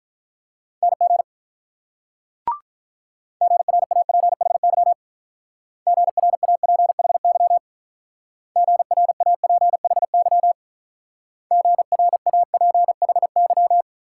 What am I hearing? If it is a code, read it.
Morse code, heard as a keyed tone, about 43 wpm, 700 Hz. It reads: NG GRAPHY GRAPHY GRAPHY GRAPHY
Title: Speed-Racing Multi-letter Phonemes - 35wpm